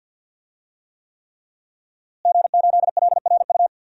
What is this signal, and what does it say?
Morse code, heard as a keyed tone, about 52 wpm, 700 Hz. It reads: G8LRU
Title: DXCC-100 Call Signs - 5 Characters - Repeated 3x - 50wpm